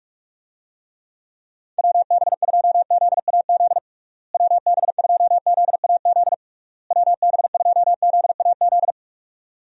Morse code, heard as a keyed tone, about 41 wpm, 700 Hz. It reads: WB2ZAZ WB2ZAZ WB2ZAZ
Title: USA Call Signs - 6 Characters - Repeated 3x - 45wpm